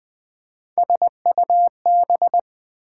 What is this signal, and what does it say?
Morse code, heard as a keyed tone, about 20 wpm, 700 Hz. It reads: SUB
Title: Build A Word - 3 Letters At A Time - 9-Letter Words - 20wpm